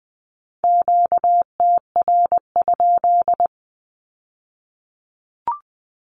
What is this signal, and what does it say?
Morse code, heard as a keyed tone, about 19 wpm, 700 Hz. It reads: QTR?
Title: Q-Codes with Spoken Meaning - 20wpm